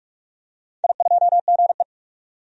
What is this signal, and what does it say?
Morse code, heard as a keyed tone, about 46 wpm, 700 Hz. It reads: I2GE
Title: DXCC-100 Call Signs - 4 Characters - 45wpm